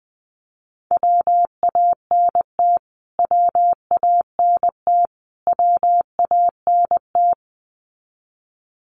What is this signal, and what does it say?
Morse code, heard as a keyed tone, about 20 wpm, 700 Hz. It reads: WANT WANT WANT
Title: Top 500 Words Repeated 20wpm